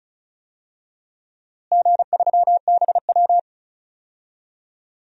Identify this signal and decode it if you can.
Morse code, heard as a keyed tone, about 34 wpm, 700 Hz. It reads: G3BW
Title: DXCC-100 Call Signs - 4 Characters - Repeated 3x - 35wpm